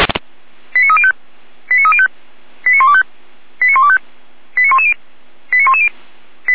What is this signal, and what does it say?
Five-tone selective call (ZVEI), heard as quick runs of five tones, tones of about 70 ms. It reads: 89305 89305 89125 89125 89110 89110